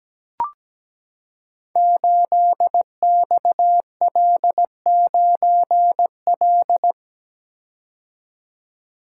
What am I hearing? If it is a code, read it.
Morse code, heard as a keyed tone, about 17 wpm, 700 Hz. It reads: 8XL9L
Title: Code Groups - 5 Letter:Numbers - 17wpm